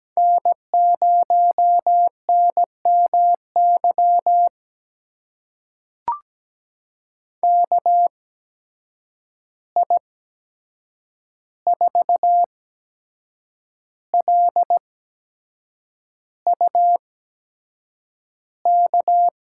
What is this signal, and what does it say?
Morse code, heard as a keyed tone, about 17 wpm, spaced as for 7 wpm, 700 Hz. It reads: N0NMY KI4LU K
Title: USA Call Signs - 8x Character Spacing - 17wpm